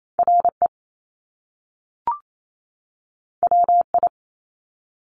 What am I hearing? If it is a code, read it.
Morse code, heard as a keyed tone, about 28 wpm, 700 Hz. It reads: R E W I
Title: Top Two Letter Combinations in Words 28wpm